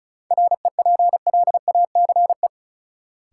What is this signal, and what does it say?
Morse code, heard as a keyed tone, about 36 wpm, 700 Hz. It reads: REPLACE
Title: Sentences from Top 1400 Words - New Word Drill - 35wpm